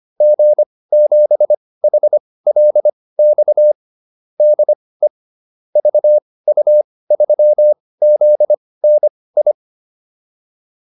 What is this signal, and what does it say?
Morse code, heard as a keyed tone, about 25 wpm, spaced as for 18 wpm, 600 Hz. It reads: G7HLX DE VU3ZNI